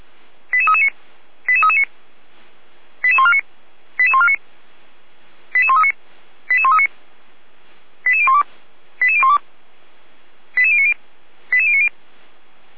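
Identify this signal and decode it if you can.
Five-tone selective call (ZVEI), heard as quick runs of five tones, tones of about 70 ms. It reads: 88339 88339 88149 88149 88139 88139 88012 88012 88090 88090
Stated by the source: ZVEI five-tone selective call received on SDR, logged by feuerwehr-orlamuende (fire brigade Orlamünde).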